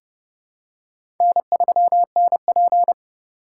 Morse code, heard as a keyed tone, about 31 wpm, 700 Hz. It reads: N3NP
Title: USA Call Signs - 4 Characters - 4x Character Spacing - 30wpm